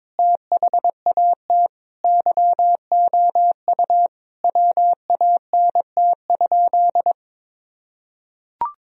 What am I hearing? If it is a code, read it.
Morse code, heard as a keyed tone, about 22 wpm, 700 Hz. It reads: THAT YOU WANT?
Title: Sentences from Top 1000 Words - Review Part 4 - 22wpm